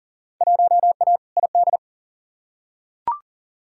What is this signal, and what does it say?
Morse code, heard as a keyed tone, about 40 wpm, 700 Hz. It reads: 1A ID
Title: Field Day Exchanges - 40wpm